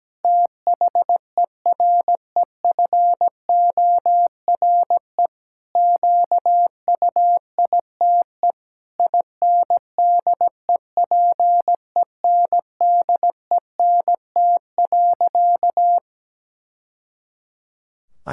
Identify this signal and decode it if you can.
Morse code, heard as a keyed tone, about 17 wpm, 700 Hz. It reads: THEREFORE QUITE INDEPENDENT.
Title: Sentences from Top 1000 Words - Review Part 6 - 17wpm